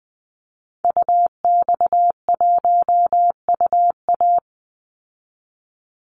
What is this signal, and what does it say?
Morse code, heard as a keyed tone, about 20 wpm, 700 Hz. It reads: UX1UA
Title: Call Signs - 20wpm